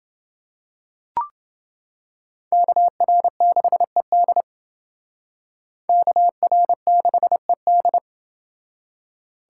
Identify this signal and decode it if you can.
Morse code, heard as a keyed tone, about 28 wpm, 700 Hz. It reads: KR6ED KR6ED